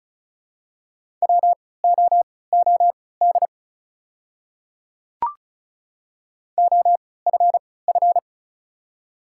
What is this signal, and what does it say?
Morse code, heard as a keyed tone, about 36 wpm, spaced as for 20 wpm, 690 Hz. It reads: WOOD OFF
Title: 3x Character Spacing - Top 300 Words - 35wpm